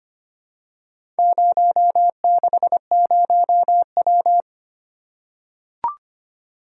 Morse code, heard as a keyed tone, about 25 wpm, 700 Hz. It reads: 060W
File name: practice / 25wpm / Four Letter-Number Combinations - 25wpm